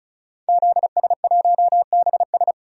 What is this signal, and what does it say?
Morse code, heard as a keyed tone, about 36 wpm, 700 Hz. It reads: ZS1BS
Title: DXCC-100 Call Signs - 5 Characters - 35wpm